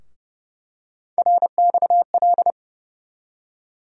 Morse code, heard as a keyed tone, about 30 wpm, 700 Hz. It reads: RXL
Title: USA Call Sign Postfixes - 3 Letters 30wpm